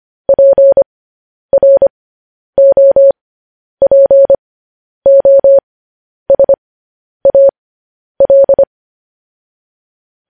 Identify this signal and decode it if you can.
Morse code, heard as a keyed tone, about 25 wpm, spaced as for 10 wpm, 550 Hz. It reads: PROPOSAL
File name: podcast / News headlines-1x-25x10